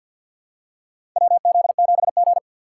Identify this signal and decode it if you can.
Morse code, heard as a keyed tone, about 52 wpm, 700 Hz. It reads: WZ7G
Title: USA Call Signs - 4 Characters - 6x Character Spacing - 50wpm